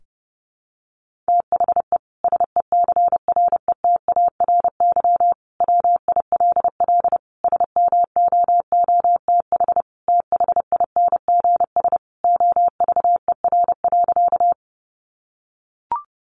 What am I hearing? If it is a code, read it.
Morse code, heard as a keyed tone, about 30 wpm, 700 Hz. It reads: THE SECRETARY WILL SMOOTH THINGS OVER.
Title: Sentences from Top 1400 Words 30wpm